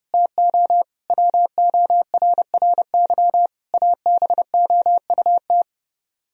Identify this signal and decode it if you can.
Morse code, heard as a keyed tone, about 30 wpm, 700 Hz. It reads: TO WORRY ABOUT
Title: Speed-Racing - Sets of 5 Words 30wpm